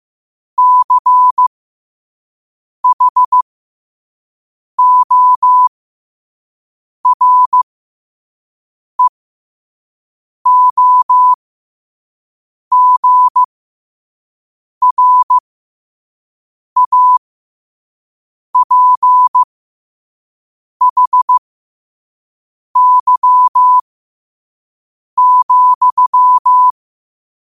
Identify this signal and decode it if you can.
Morse code, heard as a keyed tone, about 15 wpm, spaced as for 5 wpm, 1000 Hz. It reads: CHOREOGRAPHY,
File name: qotdfiles / 5wpm